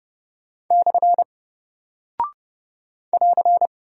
Morse code, heard as a keyed tone, about 30 wpm, 700 Hz. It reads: / <AR>
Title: Single Character (Letter:Number:Punctuation:Prosign) Alt 30wpm